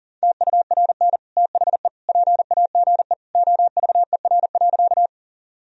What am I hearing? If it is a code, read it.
Morse code, heard as a keyed tone, about 37 wpm, 700 Hz. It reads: TURN THE PAGE OVER.
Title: Sentences from Top 1000 Words - Review Part 1 - 40wpm